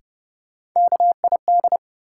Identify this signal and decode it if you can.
Morse code, heard as a keyed tone, about 30 wpm, 700 Hz. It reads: KID
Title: Sentences from Top 1600 Words - New Word Drill Alt - 30wpm